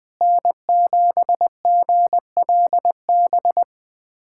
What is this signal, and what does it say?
Morse code, heard as a keyed tone, about 20 wpm, 700 Hz. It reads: N7GLB
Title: USA Callsign DE Callsign - 5 Characters - 20wpm